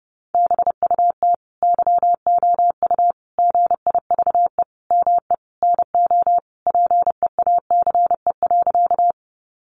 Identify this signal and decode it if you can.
Morse code, heard as a keyed tone, about 30 wpm, 700 Hz. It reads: BUT YOU GIVE ME NO PEACE.
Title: Sentences from Top 1500 Words 30wpm